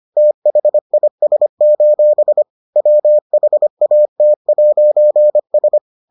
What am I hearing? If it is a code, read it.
Morse code, heard as a keyed tone, about 25 wpm, 600 Hz. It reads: THIS: WHAT'S